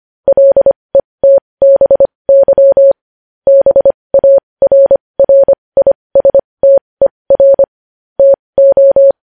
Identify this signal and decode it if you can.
Morse code, heard as a keyed tone, about 25 wpm, spaced as for 20 wpm, 550 Hz. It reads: LETBY BARRISTER TO
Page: news headlines in morse code at 20 wpm